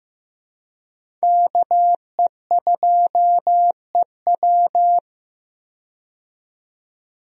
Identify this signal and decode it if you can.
Morse code, heard as a keyed tone, about 15 wpm, 700 Hz. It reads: KE2EW
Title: USA Call Signs - 5 Characters - 15wpm